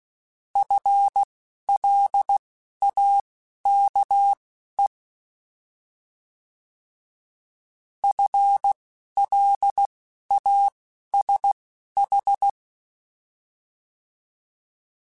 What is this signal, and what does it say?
Morse code, heard as a keyed tone, about 16 wpm, spaced as for 11 wpm, 800 Hz. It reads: FLAKE FLASH
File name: Alphabetical / Alphabetical mp3s / Fs @10